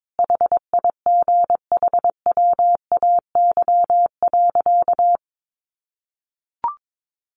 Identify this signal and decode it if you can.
Morse code, heard as a keyed tone, about 22 wpm, 700 Hz. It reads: HIGHWAY.
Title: Sentences from Top 1400 Words 22wpm